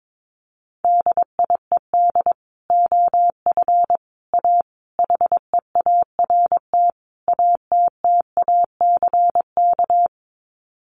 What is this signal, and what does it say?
Morse code, heard as a keyed tone, about 22 wpm, 700 Hz. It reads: DIED OF A HEART ATTACK
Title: Sets of 5 Words Encore2 22wpm